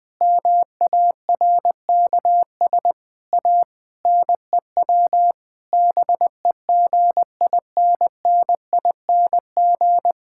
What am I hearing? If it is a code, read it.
Morse code, heard as a keyed tone, about 20 wpm, 700 Hz. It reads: MARKS A NEW BEGINNING
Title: Difficult Sentences with Hints - 20wpm